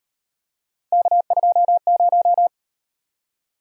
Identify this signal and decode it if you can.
Morse code, heard as a keyed tone, about 37 wpm, 700 Hz. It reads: K20